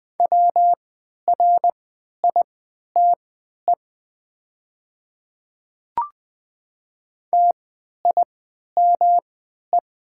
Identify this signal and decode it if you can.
Morse code, heard as a keyed tone, about 20 wpm, spaced as for 11 wpm, 700 Hz. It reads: WRITE TIME
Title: 3x Character Spacing - Top 200 Words - 20wpm